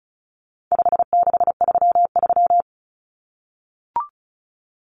Morse code, heard as a keyed tone, about 34 wpm, 700 Hz. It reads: 5633
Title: Numbers - 4-Digit 35wpm